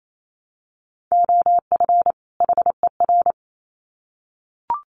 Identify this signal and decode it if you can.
Morse code, heard as a keyed tone, about 28 wpm, 700 Hz. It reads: OF HER
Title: Sets of 2 Words 28wpm